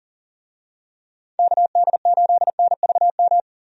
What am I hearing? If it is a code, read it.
Morse code, heard as a keyed tone, about 37 wpm, 700 Hz. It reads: KD8NVM
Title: USA Call Signs - 6 Characters - Repeated 2x - 40wpm